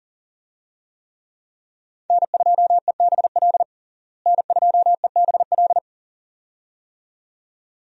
Morse code, heard as a keyed tone, about 40 wpm, 700 Hz. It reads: N2EBL N2EBL
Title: USA Call Signs - 5 Characters - Repeated 2x - 40wpm